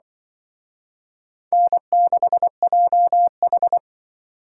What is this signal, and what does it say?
Morse code, heard as a keyed tone, about 23 wpm, 700 Hz. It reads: N6JH
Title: Speed Racing - USA 1x2 Call Signs 20wpm